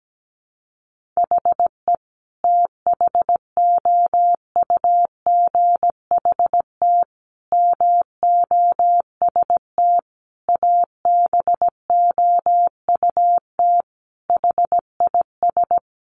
Morse code, heard as a keyed tone, about 17 wpm, 700 Hz. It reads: HE THOUGHT MOST ABOUT HIS
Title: Sentences from Top 1500 Words 17wpm